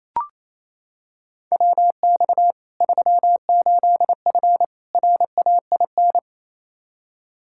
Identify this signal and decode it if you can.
Morse code, heard as a keyed tone, about 27 wpm, 700 Hz. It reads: WX 38F RAIN
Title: Practice QSO - Exchange 2 - Info - 28wpm